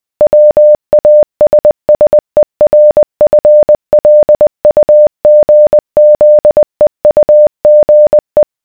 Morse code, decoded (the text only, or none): WASSERFLUGZEUGE